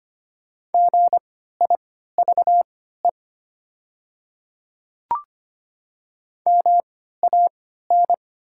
Morse code, heard as a keyed tone, about 25 wpm, spaced as for 14 wpm, 700 Hz. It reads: GIVE MAN
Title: 3x Character Spacing - Top 200 Words - 25wpm